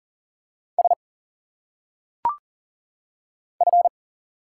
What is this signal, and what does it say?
Morse code, heard as a keyed tone, about 40 wpm, 700 Hz. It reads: S F
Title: Single Letters - Phonetic Pronunciation 40wpm